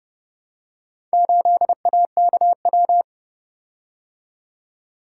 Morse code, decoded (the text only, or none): 8AKW